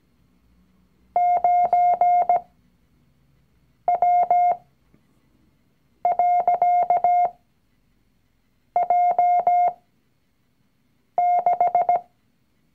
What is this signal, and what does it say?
Morse code, heard as a keyed tone, about 17 wpm, spaced as for 5 wpm, 710 Hz. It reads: 9W.J6
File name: CW-Online-Kurs / Anfänger / Lektion 019